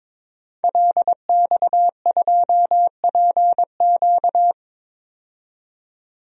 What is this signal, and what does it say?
Morse code, heard as a keyed tone, about 22 wpm, 700 Hz. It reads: LX2PQ
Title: DXCC-100 Call Signs - 5 Characters - 22wpm